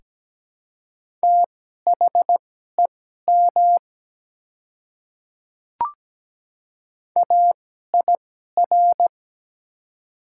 Morse code, heard as a keyed tone, about 17 wpm, spaced as for 12 wpm, 700 Hz. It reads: THEM AIR
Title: 2x Character Spacing - Top 200 Words - 17wpm